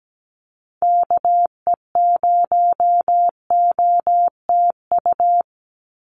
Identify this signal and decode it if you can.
Morse code, heard as a keyed tone, about 17 wpm, 700 Hz. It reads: KE0OTU
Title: USA Call Signs 17wpm